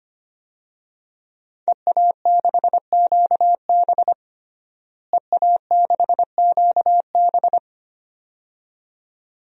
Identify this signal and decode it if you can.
Morse code, heard as a keyed tone, about 25 wpm, 700 Hz. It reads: EA6QB EA6QB
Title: DXCC-100 Call Signs - 5 Characters - Repeated 2x - 25wpm